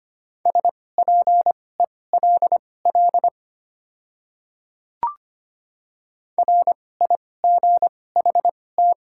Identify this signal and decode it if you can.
Morse code, heard as a keyed tone, about 25 wpm, spaced as for 18 wpm, 700 Hz. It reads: SPELL RIGHT
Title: 2x Character Spacing - Top 200 Words - 25wpm